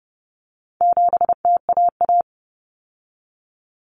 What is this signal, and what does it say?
Morse code, heard as a keyed tone, about 30 wpm, 700 Hz. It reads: 7TAA
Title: Common Cut Numbers - 4-Digit 30wpm